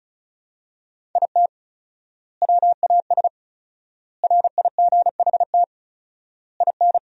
Morse code, heard as a keyed tone, about 34 wpm, 700 Hz. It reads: IT WAS RIGHT IN
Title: 4x Word-Spacing - Sentences from Top 500 Words - 35wpm